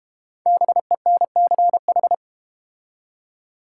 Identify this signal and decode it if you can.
Morse code, heard as a keyed tone, about 30 wpm, 690 Hz. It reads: BENCH
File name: Easy Words - Speed-Racing with Hints - 25wpm